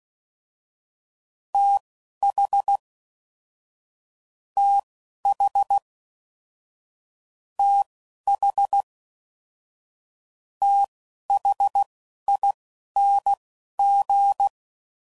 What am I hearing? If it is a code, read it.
Morse code, heard as a keyed tone, about 16 wpm, spaced as for 12 wpm, 800 Hz. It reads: TH TH TH THING
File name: Basic Sets / QST de WA7RSO / Combnations